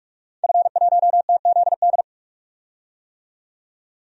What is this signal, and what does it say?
Morse code, heard as a keyed tone, about 44 wpm, 690 Hz. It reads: F1TZD
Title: DXCC-100 Call Signs - 5 Characters - 45wpm